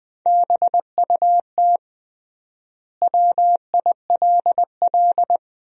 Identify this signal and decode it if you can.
Morse code, heard as a keyed tone, about 20 wpm, 700 Hz. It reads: BUT WILL
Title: 3x Word-Spacing - Sentences from Top 100 Words - 20wpm